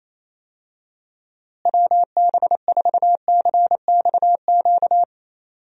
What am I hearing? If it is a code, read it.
Morse code, heard as a keyed tone, about 28 wpm, 700 Hz. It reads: WB4CXQ
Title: USA Call Signs - 6 Characters - Repeated 2x - 28wpm